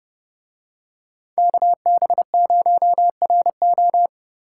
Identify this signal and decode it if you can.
Morse code, heard as a keyed tone, about 28 wpm, 700 Hz. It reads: KB0RO